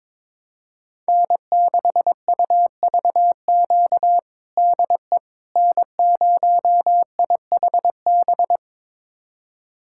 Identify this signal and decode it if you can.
Morse code, heard as a keyed tone, about 22 wpm, 700 Hz. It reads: N6UVQ DE N0IHB
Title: USA Callsign DE Callsign - 5 Characters - 22wpm